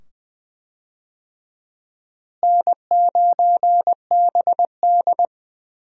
Morse code, heard as a keyed tone, about 20 wpm, 700 Hz. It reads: N9BD
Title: USA Call Signs - 4 Characters - Repeated 2x - 20wpm